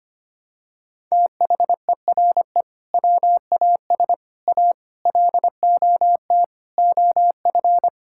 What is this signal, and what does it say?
Morse code, heard as a keyed tone, about 25 wpm, 700 Hz. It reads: THERE WAS A LOT OF
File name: practice / 25wpm / Sentences with 4-Word Phrases - 25wpm